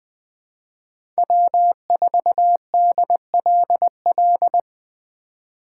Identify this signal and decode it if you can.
Morse code, heard as a keyed tone, about 20 wpm, 700 Hz. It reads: W4DLL
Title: Speed Racing - USA Call Signs 17wpm